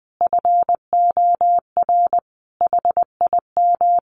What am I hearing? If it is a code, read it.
Morse code, heard as a keyed tone, about 20 wpm, 700 Hz. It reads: FOR HIM
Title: Sentences from Top 1300 Words 20wpm